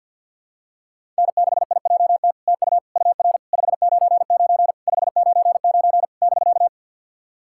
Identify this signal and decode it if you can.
Morse code, heard as a keyed tone, about 50 wpm, 700 Hz. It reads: N6IJT TU UR 599 599 <BK>
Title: SOTA QSO Practice 50wpm